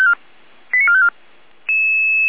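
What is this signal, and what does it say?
Five-tone selective call (ZVEI), heard as quick runs of five tones, tones of about 70 ms. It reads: 89454
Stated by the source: ZVEI five-tone selective call 85386 received via SDR.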